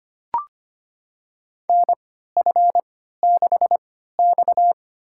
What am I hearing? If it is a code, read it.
Morse code, heard as a keyed tone, about 25 wpm, spaced as for 14 wpm, 700 Hz. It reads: NF6X
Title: USA Call Signs - 4 Characters - 3x Character Spacing - 25wpm